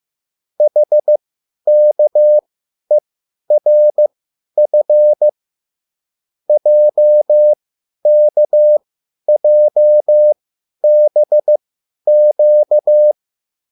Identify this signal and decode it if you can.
Morse code, heard as a keyed tone, about 15 wpm, spaced as for 10 wpm, 600 Hz. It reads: HKERF JKJBQ
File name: QSL / CW lesson